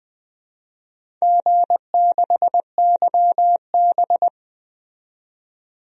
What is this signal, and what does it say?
Morse code, heard as a keyed tone, about 20 wpm, 700 Hz. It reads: G6YB
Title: DXCC-100 Call Signs - 4 Characters - 20wpm